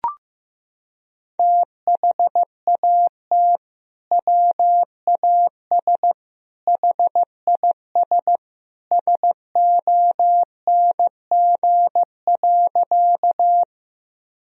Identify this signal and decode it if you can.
Morse code, heard as a keyed tone, about 15 wpm, 700 Hz. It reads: THAT WAS HIS SONG.
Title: Sentences from Top 1000 Words - Review Part 2 - 15wpm